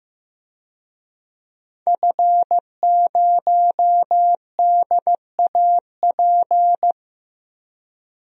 Morse code, heard as a keyed tone, about 15 wpm, 700 Hz. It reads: F0DAP
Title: DXCC-100 Call Signs - 5 Characters - Repeated 2x - 15wpm